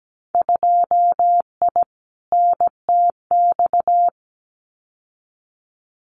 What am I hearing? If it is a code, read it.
Morse code, heard as a keyed tone, about 17 wpm, 700 Hz. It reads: 2I NTX
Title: Winter Field Day Exchanges Encore 17wpm